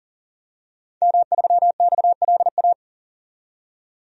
Morse code, heard as a keyed tone, about 40 wpm, 700 Hz. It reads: M3XLA
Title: DXCC-100 Call Signs - 5 Characters - 40wpm